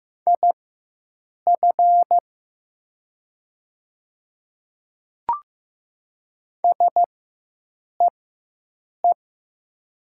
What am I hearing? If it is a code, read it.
Morse code, heard as a keyed tone, about 15 wpm, spaced as for 7 wpm, 700 Hz. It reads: IF SEE
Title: 4x Character Spacing - Top 100 Words - 15wpm